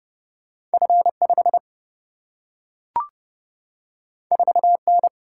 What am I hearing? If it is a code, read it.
Morse code, heard as a keyed tone, about 30 wpm, 700 Hz. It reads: F 5 4 N
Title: Single-Letter Single-Number 30wpm